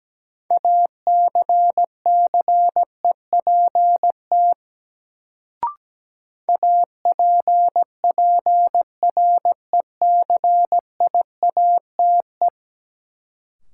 Morse code, heard as a keyed tone, about 17 wpm, 700 Hz. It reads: ACCEPT APPRECIATE
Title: Sentences from Top 1300 Words - New Word Drill Alt - 17wpm